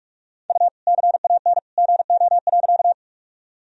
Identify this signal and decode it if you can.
Morse code, heard as a keyed tone, about 46 wpm, 700 Hz. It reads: U CAN GO.
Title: Sentences from Top 100 Words and Abbreviations 45wpm